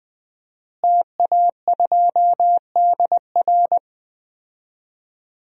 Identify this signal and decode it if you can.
Morse code, heard as a keyed tone, about 20 wpm, 700 Hz. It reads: TA2DR
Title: DXCC-100 Call Signs - 5 Characters - Repeated 2x - 20wpm